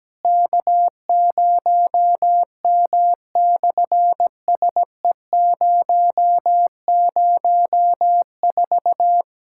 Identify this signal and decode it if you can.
Morse code, heard as a keyed tone, about 17 wpm, 700 Hz. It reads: K0M/SE004
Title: SOTA QSO Practice 17wpm